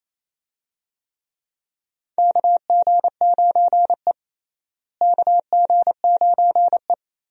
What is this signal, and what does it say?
Morse code, heard as a keyed tone, about 28 wpm, 700 Hz. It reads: KG9E KG9E
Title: USA Call Signs - 4 Characters - Repeated 2x - 28wpm